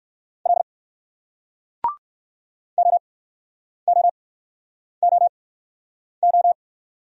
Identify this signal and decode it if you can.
Morse code, heard as a keyed tone, about 60 wpm, spaced as for 9 wpm, 700 Hz. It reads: L OOOO